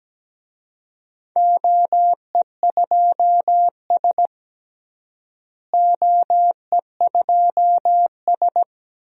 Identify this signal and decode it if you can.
Morse code, heard as a keyed tone, about 17 wpm, 700 Hz. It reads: OE2S OE2S
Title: DXCC-100 Call Signs - 4 Characters - Repeated 3x - 17wpm